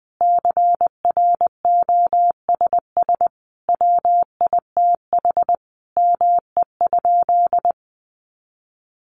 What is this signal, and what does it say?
Morse code, heard as a keyed tone, about 20 wpm, 700 Hz. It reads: CROSS WITH ME?
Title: Sentences from Top 300 Words 20wpm